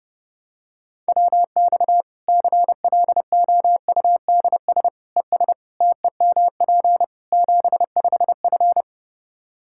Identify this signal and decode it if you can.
Morse code, heard as a keyed tone, about 32 wpm, 700 Hz. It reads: WX CLOUDS ES TEMP 75F
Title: K1USN SST QSO Practice 30wpm